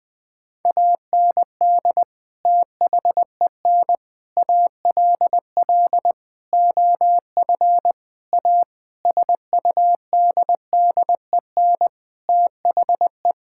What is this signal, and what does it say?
Morse code, heard as a keyed tone, about 20 wpm, 700 Hz. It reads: AND THEN ALL OF A SUDDEN THE
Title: Sentences with 6-Word Phrases - 20wpm